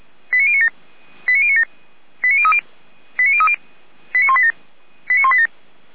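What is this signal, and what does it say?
Five-tone selective call (ZVEI), heard as quick runs of five tones, tones of about 70 ms. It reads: 89097 89097 89030 89030 89187 89187
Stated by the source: Five-tone ZVEI selective call received on an SDR.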